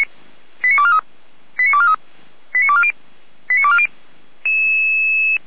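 Five-tone selective call (ZVEI), heard as quick runs of five tones, tones of about 70 ms. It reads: 89243 89243 89240 89240
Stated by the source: Five-tone ZVEI call received on an SDR, fire brigade Feuerwehr Orlamünde.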